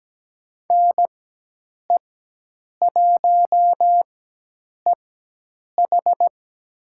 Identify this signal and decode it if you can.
Morse code, heard as a keyed tone, about 17 wpm, spaced as for 8 wpm, 700 Hz. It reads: NE1EH
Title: USA Call Signs - 5 Characters - 4x Character Spacing - 17wpm